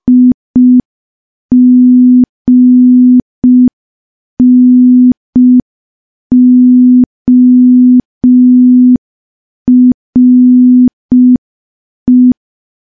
Morse code, heard as a keyed tone, about 5 wpm, 250 Hz. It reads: IGNORE
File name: morse / morse-training-05WPM